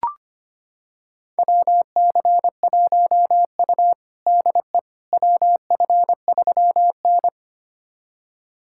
Morse code, heard as a keyed tone, about 25 wpm, 700 Hz. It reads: WC1U DE WF3N
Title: USA Callsign DE Callsign - 4 Characters - 25wpm